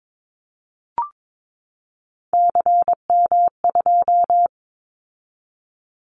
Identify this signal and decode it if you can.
Morse code, heard as a keyed tone, about 22 wpm, 700 Hz. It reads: CM2
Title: Callsign Prefixes for Top 100 DXCC - 22wpm